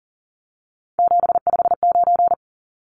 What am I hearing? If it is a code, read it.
Morse code, heard as a keyed tone, about 40 wpm, 700 Hz. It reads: 759
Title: Numbers - 3-Digit 40wpm